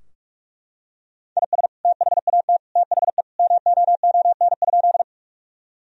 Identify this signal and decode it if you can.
Morse code, heard as a keyed tone, about 41 wpm, 700 Hz. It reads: IS THAT THE MOON?
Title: Sentences from Top 1000 Words - Review Part 3 - 45wpm